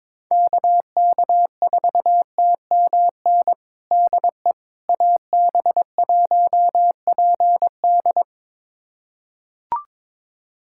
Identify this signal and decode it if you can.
Morse code, heard as a keyed tone, about 22 wpm, 700 Hz. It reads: KK4TMN DE AB1PD
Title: USA Callsign DE Callsign 22wpm